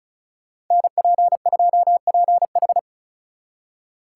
Morse code, heard as a keyed tone, about 47 wpm, spaced as for 41 wpm, 700 Hz. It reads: NP2PH
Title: Speed Racing - USA 2x2 Call Signs 35wpm